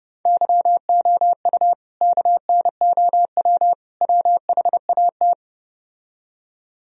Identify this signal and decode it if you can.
Morse code, heard as a keyed tone, about 30 wpm, 700 Hz. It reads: YOU KNOW WHAT